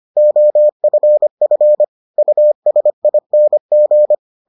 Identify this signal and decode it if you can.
Morse code, heard as a keyed tone, about 25 wpm, 600 Hz. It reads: OFF USING